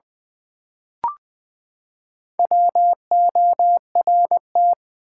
Morse code, heard as a keyed tone, about 20 wpm, 700 Hz. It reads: WORT
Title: Build A Word - 4 Letters At A Time - 10-Letter Words - 20wpm